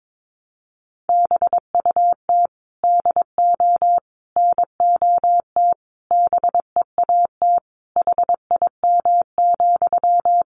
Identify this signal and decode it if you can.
Morse code, heard as a keyed tone, about 22 wpm, 700 Hz. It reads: BUT DO NOT BEAT HIM,
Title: Sentences from Top 1300 Words 22wpm